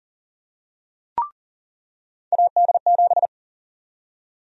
Morse code, heard as a keyed tone, about 40 wpm, 700 Hz. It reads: AD7
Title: USA Call Sign Prefixes - 40wpm